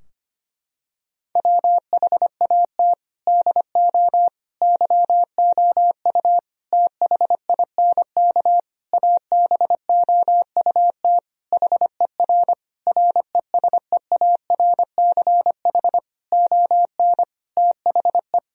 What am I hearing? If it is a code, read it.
Morse code, heard as a keyed tone, about 25 wpm, 700 Hz. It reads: WHAT DO YOU THINK ABOUT HER RESEARCH ON THE
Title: Sentences from Top 1000 Words - Review Part 4 - 25wpm